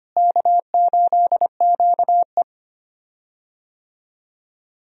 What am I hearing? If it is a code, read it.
Morse code, heard as a keyed tone, about 25 wpm, 700 Hz. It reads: K8QE
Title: USA Call Signs - 4 Characters - Repeated 2x - 25wpm